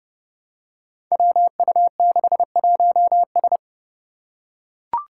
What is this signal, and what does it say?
Morse code, heard as a keyed tone, about 30 wpm, 700 Hz. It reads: WU61S
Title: Five Challenging Letter-Number Combinations 30wpm